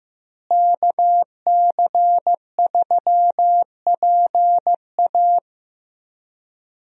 Morse code, heard as a keyed tone, about 15 wpm, 700 Hz. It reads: KC3PA
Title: USA Call Signs - 5 Characters - Repeated 3x - 15wpm